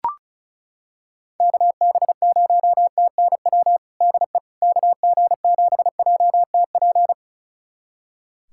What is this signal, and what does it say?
Morse code, heard as a keyed tone, about 34 wpm, 700 Hz. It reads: KB0TNW DE KG7JTP
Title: USA Callsign DE Callsign - 6 Characters - 35wpm